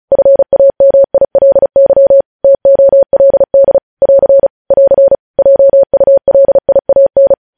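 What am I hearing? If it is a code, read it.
Morse code, heard as a keyed tone, about 36 wpm, 550 Hz. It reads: FAMILY TOLD <AR> <AR> JULIAN
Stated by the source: News headlines sent in Morse at 35 wpm.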